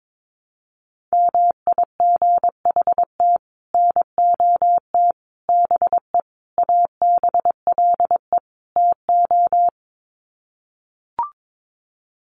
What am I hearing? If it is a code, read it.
Morse code, heard as a keyed tone, about 22 wpm, 700 Hz. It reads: MIGHT NOT BE ABLE TO
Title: Sets of 5 Words Encore2 22wpm